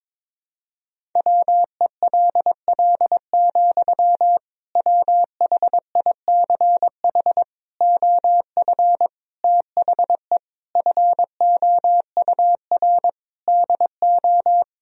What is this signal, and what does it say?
Morse code, heard as a keyed tone, about 22 wpm, 700 Hz. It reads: WELL, WHICH OF THE FOUR DO
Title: Sentences from Top 1000 Words - Review Part 1 - 22wpm